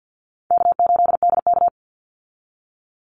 Morse code, heard as a keyed tone, about 50 wpm, 700 Hz. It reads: K8DK
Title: Call Signs - 50wpm